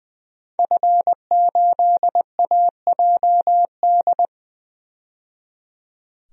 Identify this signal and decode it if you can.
Morse code, heard as a keyed tone, about 20 wpm, 700 Hz. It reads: F8AJD
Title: DXCC-100 Call Signs - 5 Characters - 20wpm